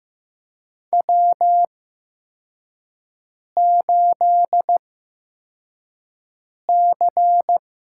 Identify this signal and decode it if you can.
Morse code, heard as a keyed tone, about 15 wpm, spaced as for 4 wpm, 700 Hz. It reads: W8C
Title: USA Call Signs - 5 Characters - 8x Character Spacing - 15wpm